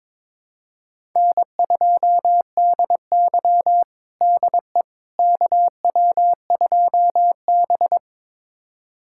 Morse code, heard as a keyed tone, about 22 wpm, 700 Hz. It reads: N2DY DE KW2B
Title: USA Callsign DE Callsign - 4 Characters - 22wpm